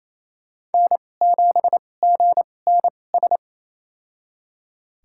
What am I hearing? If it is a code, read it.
Morse code, heard as a keyed tone, about 28 wpm, spaced as for 20 wpm, 700 Hz. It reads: N7GNS
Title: USA Call Signs - 5 Characters - 2x Character Spacing - 28wpm